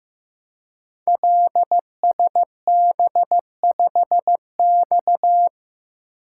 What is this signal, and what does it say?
Morse code, heard as a keyed tone, about 15 wpm, 700 Hz. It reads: LSB5X
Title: Code Groups - 5 Letter:Numbers - 15wpm